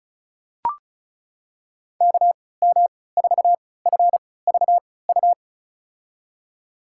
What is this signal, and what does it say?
Morse code, heard as a keyed tone, about 36 wpm, spaced as for 20 wpm, 700 Hz. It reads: KM4FVU
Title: USA Call Signs - 6 Characters - 3x Character Spacing - 35wpm